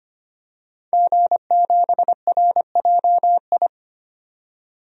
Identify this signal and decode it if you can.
Morse code, heard as a keyed tone, about 25 wpm, 700 Hz. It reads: G7RJI
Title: DXCC-100 Call Signs - 5 Characters - 25wpm